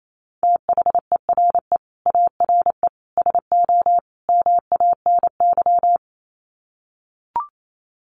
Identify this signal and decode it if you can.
Morse code, heard as a keyed tone, about 28 wpm, 700 Hz. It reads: THERE ARE SO MANY
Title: Sets of 4 Words 28wpm